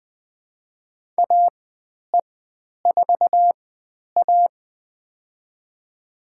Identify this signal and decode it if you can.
Morse code, heard as a keyed tone, about 20 wpm, spaced as for 10 wpm, 700 Hz. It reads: AE4A
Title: 095 Lesson 16g - Letter c - Call Signs 20wpm